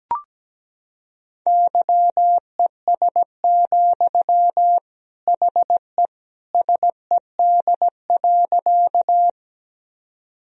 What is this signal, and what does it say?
Morse code, heard as a keyed tone, about 17 wpm, 700 Hz. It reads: YES, HE SED.
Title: Sentences from Top 500 Words and Abbreviations 17wpm